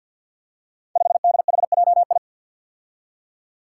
Morse code, heard as a keyed tone, about 47 wpm, 700 Hz. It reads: 5DHJI
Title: Five Challenging Letter-Number Combinations 50wpm